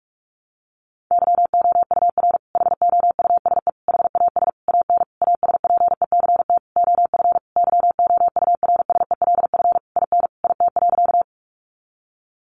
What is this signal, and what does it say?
Morse code, heard as a keyed tone, about 44 wpm, 700 Hz. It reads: YOUR HOUSE HAS AN ASPECT OF YOURSELF IN IT.
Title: Sentences from Top 1300 Words 45wpm